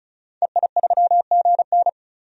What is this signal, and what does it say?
Morse code, heard as a keyed tone, about 34 wpm, 700 Hz. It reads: EI3GN
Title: DXCC-100 Call Signs - 5 Characters - 35wpm